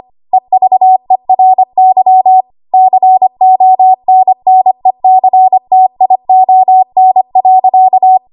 Morse code, heard as a keyed tone, about 25 wpm, 750 Hz. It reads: EVERY CONNECTION.
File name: Morse / 25WPM